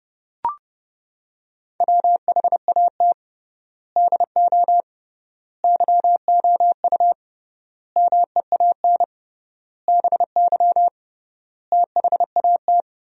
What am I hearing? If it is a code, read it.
Morse code, heard as a keyed tone, about 30 wpm, 700 Hz. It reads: WHAT DO YOU MEAN BY THAT
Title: 3x Word-Spacing - Sentences from Top 400 Words - 30wpm